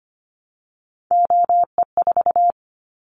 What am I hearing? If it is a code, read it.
Morse code, heard as a keyed tone, about 25 wpm, 700 Hz. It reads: OE4
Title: Callsign Prefixes for Top 25 DXCC - Repeated 3x - 25wpm